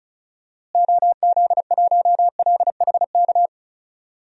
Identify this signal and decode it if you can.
Morse code, heard as a keyed tone, about 36 wpm, 700 Hz. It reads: OZ1LHK
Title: DXCC-100 Call Signs - 6 Characters - Repeated 2x - 35wpm